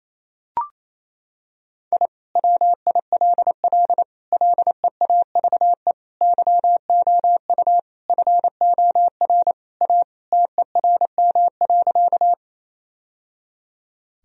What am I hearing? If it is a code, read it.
Morse code, heard as a keyed tone, about 28 wpm, 700 Hz. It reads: I WILL LEAVE YOU FOR A TERM.
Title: Sentences from Top 1000 Words - Review Part 4 - 28wpm